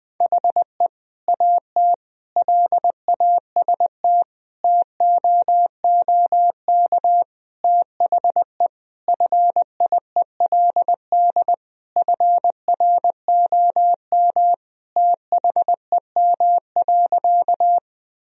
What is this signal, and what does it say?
Morse code, heard as a keyed tone, about 20 wpm, 700 Hz. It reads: HE AT LAST TOOK THE FIELD FROM THEM.
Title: Sentences from Top 1000 Words - Review Part 2 - 20wpm